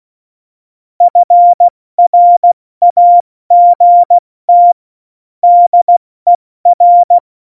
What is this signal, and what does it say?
Morse code, heard as a keyed tone, about 16 wpm, 700 Hz. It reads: FRAGT DER
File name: Morsekurs / Witze / Witz 16. Oktober 2018, 70 F80